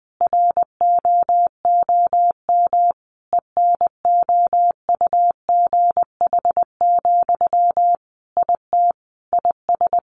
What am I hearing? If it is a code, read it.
Morse code, heard as a keyed tone, about 20 wpm, 700 Hz. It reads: ROOM ENOUGH, IT IS
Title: Sentences from Top 1400 Words 20wpm